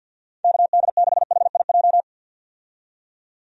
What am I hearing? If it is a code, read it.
Morse code, heard as a keyed tone, about 51 wpm, 700 Hz. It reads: KD6HIJ
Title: USA Call Signs - 6 Characters - 50wpm